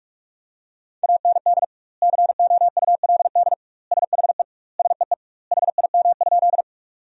Morse code, heard as a keyed tone, about 42 wpm, 700 Hz. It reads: AND COULD SHE SEE HIM?
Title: 2x Word-Spacing - Sentences from Top 100 Words - 45wpm